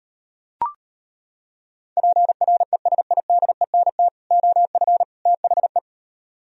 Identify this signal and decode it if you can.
Morse code, heard as a keyed tone, about 32 wpm, 700 Hz. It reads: PRESIDENT OF THE